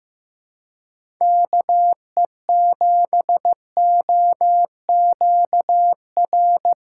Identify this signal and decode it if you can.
Morse code, heard as a keyed tone, about 15 wpm, 700 Hz. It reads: KE7OQR
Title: USA Callsign DE Callsign - 6 Characters - 15wpm